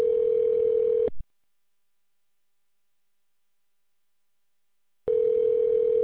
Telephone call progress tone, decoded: ringback tone

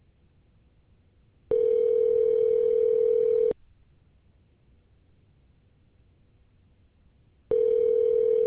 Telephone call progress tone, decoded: ringback tone

